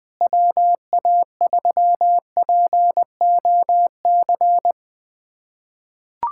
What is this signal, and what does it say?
Morse code, heard as a keyed tone, about 21 wpm, 700 Hz. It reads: WA3POC
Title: USA Callsign DE Callsign 20wpm